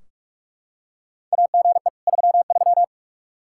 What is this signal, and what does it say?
Morse code, heard as a keyed tone, about 46 wpm, 700 Hz. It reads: AGE 33
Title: K1USN SST QSO Practice 45wpm